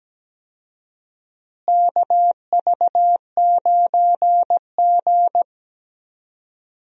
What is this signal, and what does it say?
Morse code, heard as a keyed tone, about 17 wpm, 700 Hz. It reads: KV9G
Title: USA Call Signs - 4 Characters - Repeated 3x - 17wpm